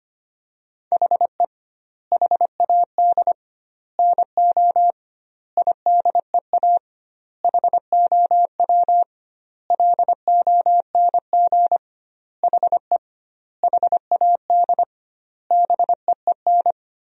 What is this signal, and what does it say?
Morse code, heard as a keyed tone, about 25 wpm, 700 Hz. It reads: HE HAD NO IDEA HOW LONG HE HAD BEEN
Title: 2x Word-Spacing - Sentences from Top 300 Words - 25wpm